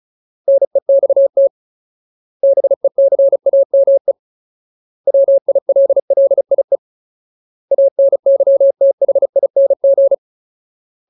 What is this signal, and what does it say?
Morse code, heard as a keyed tone, about 34 wpm, 550 Hz. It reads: NEXT BECAME WILLIE ANYTHING
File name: Ultimate Word List / 35WPM Ultimate